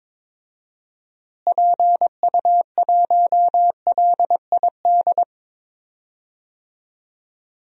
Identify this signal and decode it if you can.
Morse code, heard as a keyed tone, about 22 wpm, 700 Hz. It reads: PU1LID
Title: DXCC-100 Call Signs - 6 Characters - Repeated 3x - 22wpm